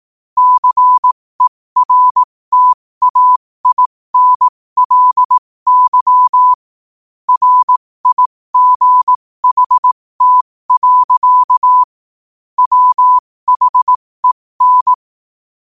Morse code, heard as a keyed tone, about 18 wpm, spaced as for 15 wpm, 1000 Hz. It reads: CERTAINLY RIGHT. WHEN
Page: New quotes every day in morse code at 15 Words per minute.